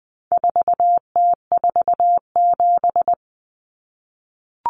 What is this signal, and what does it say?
Morse code, heard as a keyed tone, about 20 wpm, 700 Hz. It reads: 4T47
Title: Common Cut Numbers - 4-Digit 20wpm